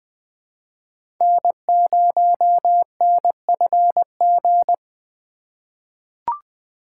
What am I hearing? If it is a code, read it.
Morse code, heard as a keyed tone, about 20 wpm, 700 Hz. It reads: N0NFG
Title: USA Call Signs - 5 Characters - 4x Character Spacing - 20wpm